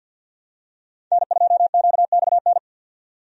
Morse code, heard as a keyed tone, about 47 wpm, 700 Hz. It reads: N2QXN
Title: USA Call Signs - 5 Characters - 50wpm